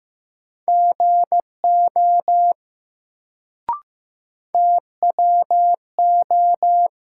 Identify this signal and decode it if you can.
Morse code, heard as a keyed tone, about 15 wpm, 700 Hz. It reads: GO TWO
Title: Sentences from Top 100 Words - New Word Drill - 15wpm